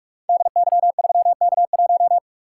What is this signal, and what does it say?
Morse code, heard as a keyed tone, about 44 wpm, 700 Hz. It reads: DY3K1
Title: Code Groups - 5 Letter:Numbers - 45wpm